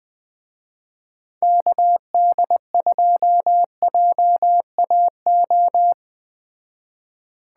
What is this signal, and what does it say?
Morse code, heard as a keyed tone, about 20 wpm, 700 Hz. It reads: KD2JAO
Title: Speed Racing - USA 2x3 Call Signs 17wpm